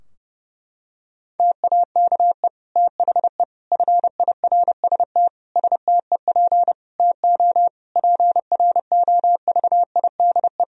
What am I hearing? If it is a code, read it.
Morse code, heard as a keyed tone, about 30 wpm, 700 Hz. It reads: TAKE THE FIRST STEP TO PROVIDE
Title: Sentences from Top 1000 Words - Review Part 4 - 30wpm